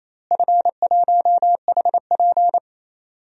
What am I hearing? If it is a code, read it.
Morse code, heard as a keyed tone, about 28 wpm, 700 Hz. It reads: F1HP
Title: DXCC-100 Call Signs - 4 Characters - Repeated 3x - 28wpm